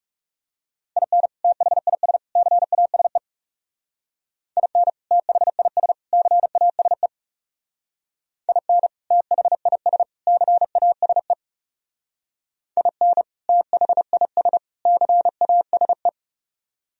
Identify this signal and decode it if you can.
Morse code, heard as a keyed tone, about 34 wpm, 700 Hz. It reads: IN THIS CASE IN THIS CASE IN THIS CASE IN THIS CASE
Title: Speed-Racing - Sets of 3 Words 30wpm